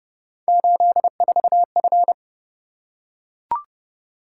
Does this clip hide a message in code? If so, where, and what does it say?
Yes: Morse code from 0.5 to 2.1 s, heard as a keyed tone, about 28 wpm, 700 Hz. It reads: 84F